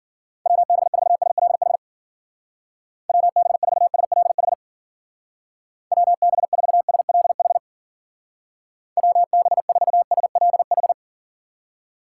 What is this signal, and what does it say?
Morse code, heard as a keyed tone, about 51 wpm, 700 Hz. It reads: WB4SLH WB4SLH WB4SLH WB4SLH